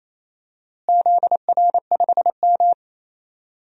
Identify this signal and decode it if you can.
Morse code, heard as a keyed tone, about 28 wpm, 700 Hz. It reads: ZR5M
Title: DXCC-100 Call Signs - 4 Characters - Repeated 3x - 28wpm